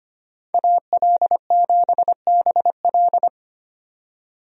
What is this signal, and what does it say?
Morse code, heard as a keyed tone, about 25 wpm, 700 Hz. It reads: AL7BL
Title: USA Call Signs - 5 Characters - 25wpm